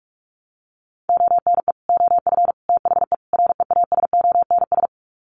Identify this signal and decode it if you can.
Morse code, heard as a keyed tone, about 44 wpm, 700 Hz. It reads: ONE OF THE REASONS
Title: Sets of 4 Words 45wpm